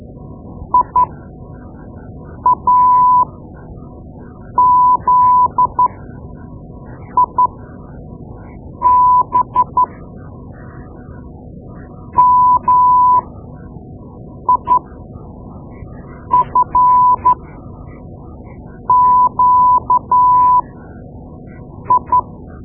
Morse code, heard as a keyed tone, about 12 wpm, spaced as for 5 wpm, 1000 Hz. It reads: IAZIB MIFQI
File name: Morse Lessons / Morse Lessons / Tape-2B